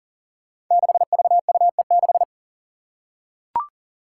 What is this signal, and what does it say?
Morse code, heard as a keyed tone, about 42 wpm, 700 Hz. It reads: 6VUE6
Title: Five Challenging Letter-Number Combinations 40wpm